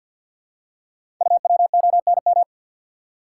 Morse code, heard as a keyed tone, about 48 wpm, 690 Hz. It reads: UWONM
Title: Code Groups - 5 Letters - 50wpm